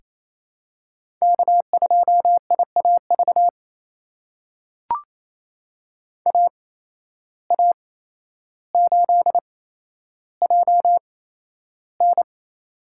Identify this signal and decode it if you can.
Morse code, heard as a keyed tone, about 28 wpm, spaced as for 12 wpm, 700 Hz. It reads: K2IAV AA8JN
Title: USA Call Signs - 5 Characters - 8x Character Spacing - 28wpm